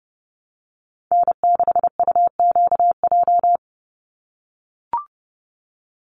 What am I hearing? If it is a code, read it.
Morse code, heard as a keyed tone, about 30 wpm, 700 Hz. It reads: N6UQJ
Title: USA Call Signs 30wpm